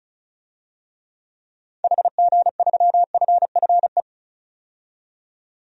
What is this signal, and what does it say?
Morse code, heard as a keyed tone, about 34 wpm, 700 Hz. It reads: HG3FFE
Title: DXCC-100 Call Signs - 6 Characters - 35wpm